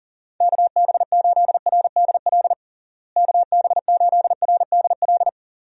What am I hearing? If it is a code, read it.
Morse code, heard as a keyed tone, about 37 wpm, 700 Hz. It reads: KB8RDL KB8RDL
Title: USA Call Signs - 6 Characters - Repeated 2x - 40wpm